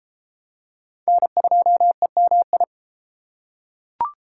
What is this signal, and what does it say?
Morse code, heard as a keyed tone, about 33 wpm, 700 Hz. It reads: N2EMI